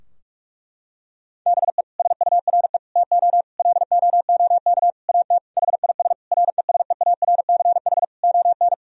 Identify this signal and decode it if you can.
Morse code, heard as a keyed tone, about 46 wpm, 700 Hz. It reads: BE SURE TO LOOK AT HIS RESEARCH ON
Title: Sentences from Top 1000 Words - Review Part 4 - 45wpm